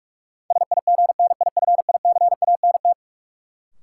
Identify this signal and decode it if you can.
Morse code, heard as a keyed tone, about 46 wpm, 700 Hz. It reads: SIGNIFICANT
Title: Sentences from Top 1600 Words - New Word Drill Alt - 45wpm